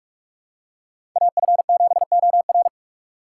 Morse code, heard as a keyed tone, about 46 wpm, 700 Hz. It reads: AF7OR
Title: USA Call Signs - 4x Character Spacing - 45wpm